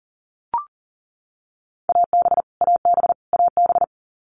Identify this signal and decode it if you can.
Morse code, heard as a keyed tone, about 40 wpm, 700 Hz. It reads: AB AB AB
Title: Canadian Provinces and Territories - Repeated 3x - 40wpm